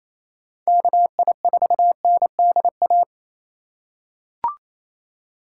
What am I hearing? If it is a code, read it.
Morse code, heard as a keyed tone, about 28 wpm, 700 Hz. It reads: KI4NDA
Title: USA Call Signs - 6 Characters - 4x Character Spacing - 28wpm